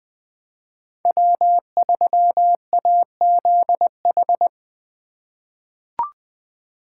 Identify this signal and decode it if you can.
Morse code, heard as a keyed tone, about 20 wpm, 700 Hz. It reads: W3AZH
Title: USA Call Signs - 5 Characters - 2x Character Spacing - 20wpm